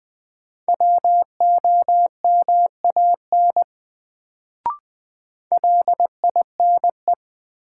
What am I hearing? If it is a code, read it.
Morse code, heard as a keyed tone, about 20 wpm, 700 Hz. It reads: WOMAN LINE
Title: Sets of 5 Words - New Word Drill - 20wpm